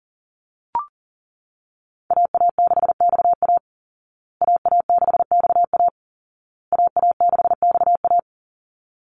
Morse code, heard as a keyed tone, about 40 wpm, 700 Hz. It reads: AA6XA AA6XA AA6XA
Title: Call Signs Repeated 2x - 40wpm